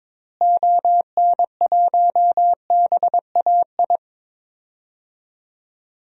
Morse code, heard as a keyed tone, about 22 wpm, 700 Hz. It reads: ON1BAI
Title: DXCC-100 Call Signs - 6 Characters - Repeated 2x - 22wpm